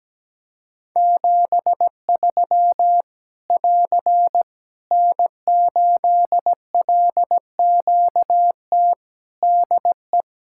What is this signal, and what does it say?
Morse code, heard as a keyed tone, about 17 wpm, 700 Hz. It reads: 73 <AR> N8LQT DE
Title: K1USN SST QSO Practice 17wpm